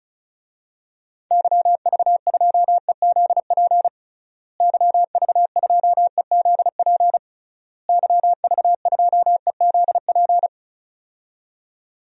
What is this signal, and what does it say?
Morse code, heard as a keyed tone, about 34 wpm, 700 Hz. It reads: YV2EZP YV2EZP YV2EZP
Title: DXCC-100 Call Signs - 6 Characters - Repeated 3x - 35wpm